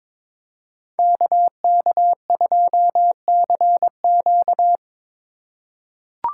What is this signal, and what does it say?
Morse code, heard as a keyed tone, about 22 wpm, 700 Hz. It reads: KK2CQ
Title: USA Call Signs - 5 Characters - 2x Character Spacing - 22wpm